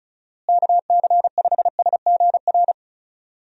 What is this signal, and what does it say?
Morse code, heard as a keyed tone, about 34 wpm, 700 Hz. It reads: KC5SGR
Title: USA Call Signs - 6 Characters - Repeated 3x - 35wpm